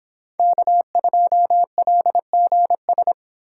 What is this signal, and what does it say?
Morse code, heard as a keyed tone, about 24 wpm, 700 Hz. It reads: K2LGS